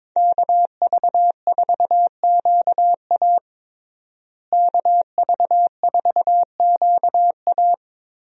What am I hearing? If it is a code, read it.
Morse code, heard as a keyed tone, about 22 wpm, 700 Hz. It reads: KV4QA KV4QA
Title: USA Call Signs - 5 Characters - Repeated 3x - 22wpm